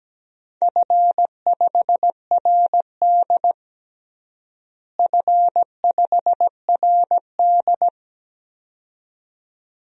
Morse code, heard as a keyed tone, about 17 wpm, 700 Hz. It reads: F5RD F5RD
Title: DXCC-100 Call Signs - 4 Characters - Repeated 2x - 17wpm